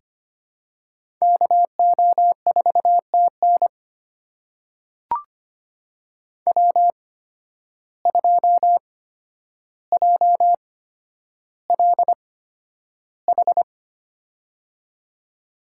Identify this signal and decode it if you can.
Morse code, heard as a keyed tone, about 25 wpm, spaced as for 11 wpm, 700 Hz. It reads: KO4TN W2JLH
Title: USA Call Signs - 8x Character Spacing - 25wpm